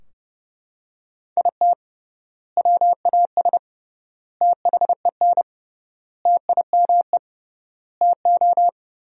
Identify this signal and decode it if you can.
Morse code, heard as a keyed tone, about 30 wpm, 700 Hz. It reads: IT WAS THEN TIME TO
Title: 3x Word-Spacing - Sentences from Top 300 Words - 30wpm